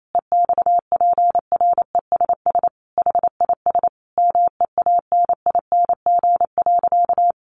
Morse code, heard as a keyed tone, about 28 wpm, 700 Hz. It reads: EXPRESS HIS MEANING.
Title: Sentences from Top 1100 Words 28wpm